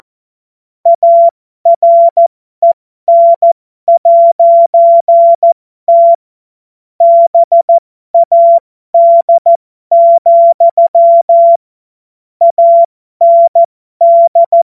Morse code, heard as a keyed tone, about 14 wpm, spaced as for 12 wpm, 680 Hz. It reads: AREN'T BAD, AND
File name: Morsekurs / Witze